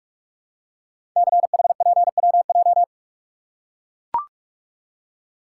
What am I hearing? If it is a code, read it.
Morse code, heard as a keyed tone, about 44 wpm, 700 Hz. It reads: CHPWJ
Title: Code Groups - 5 Letters - 45wpm